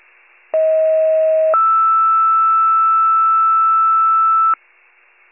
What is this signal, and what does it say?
Two-tone page, heard as a one-second tone, then a three-second tone, about 640 then 1320 Hz. It is heard from 0.5 to 4.5 s.
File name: pager / 95 RESCUE / wcalert